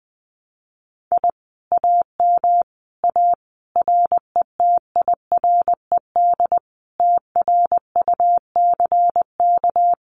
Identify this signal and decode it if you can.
Morse code, heard as a keyed tone, about 20 wpm, 700 Hz. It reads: I AM A RETIRED TRUCK
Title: Common QSO Phrases Encore 20wpm